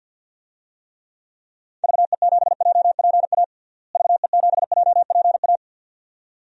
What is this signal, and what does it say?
Morse code, heard as a keyed tone, about 50 wpm, 700 Hz. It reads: VE7JPA VE7JPA
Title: DXCC-100 Call Signs - 6 Characters - Repeated 2x - 50wpm